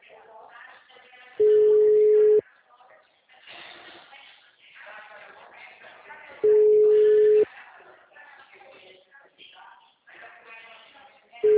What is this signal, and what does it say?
Ringback tone (Europe, 425 Hz), with speech.